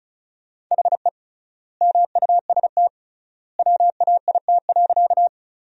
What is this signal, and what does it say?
Morse code, heard as a keyed tone, about 34 wpm, 700 Hz. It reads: HE MUST WAIT.
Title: 3x Word-Spacing - Sentences from Top 500 Words - 35wpm